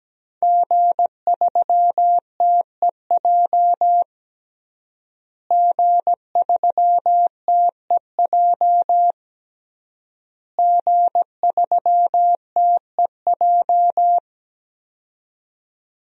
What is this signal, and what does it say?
Morse code, heard as a keyed tone, about 17 wpm, 700 Hz. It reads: G3TEJ G3TEJ G3TEJ
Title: DXCC-100 Call Signs - 5 Characters - Repeated 3x - 17wpm